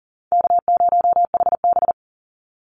Morse code, heard as a keyed tone, about 40 wpm, 700 Hz. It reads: K0HB
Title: Easy Call Signs Repeated 3x - 40wpm